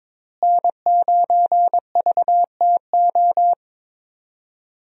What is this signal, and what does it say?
Morse code, heard as a keyed tone, about 22 wpm, 700 Hz. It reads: N9VTO
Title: USA Call Signs - 5 Characters - Repeated 3x - 22wpm